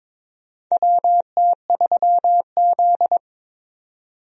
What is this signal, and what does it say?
Morse code, heard as a keyed tone, about 21 wpm, 700 Hz. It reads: WT3Z
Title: USA Call Signs - 4 Characters - 22wpm